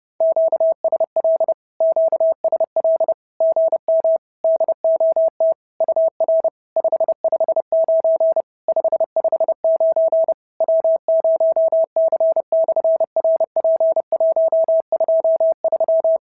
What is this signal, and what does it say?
Morse code, heard as a keyed tone, about 30 wpm, 650 Hz. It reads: QSL QSL GM DOT UR 559 559 W0C/RP123
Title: SOTA QSO Practice 30wpm